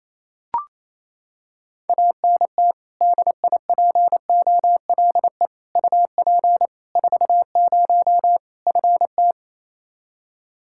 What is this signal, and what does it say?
Morse code, heard as a keyed tone, about 27 wpm, 700 Hz. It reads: ANT DIPOLE UP 40 FT
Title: Practice QSO - Exchange 2 - Info - 28wpm